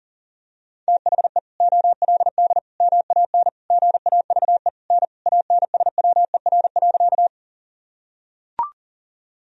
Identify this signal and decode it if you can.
Morse code, heard as a keyed tone, about 40 wpm, 700 Hz. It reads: THE OLD MAN GAVE N ANSWER.
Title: Sentences from Top 500 Words and Abbreviations 40wpm